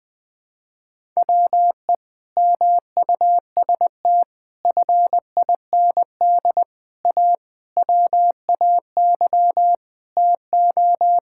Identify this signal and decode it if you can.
Morse code, heard as a keyed tone, about 20 wpm, 700 Hz. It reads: WE MUST FIND A WAY TO
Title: Sentences with 4-Word Phrases - Spoken Beginning - 20wpm